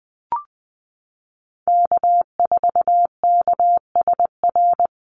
Morse code, heard as a keyed tone, about 20 wpm, 700 Hz. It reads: K4KSR
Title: Call Signs Repeated 2x - 20wpm